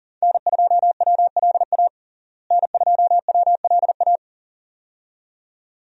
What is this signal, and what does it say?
Morse code, heard as a keyed tone, about 40 wpm, 700 Hz. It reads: N2WLA N2WLA
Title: USA Call Signs - 5 Characters - Repeated 2x - 40wpm